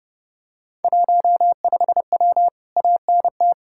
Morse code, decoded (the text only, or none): 15W ANT